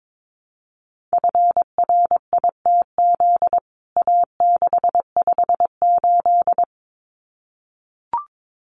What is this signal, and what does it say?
Morse code, heard as a keyed tone, about 22 wpm, 700 Hz. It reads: FRITZ A658
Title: CWT Exchange 22wpm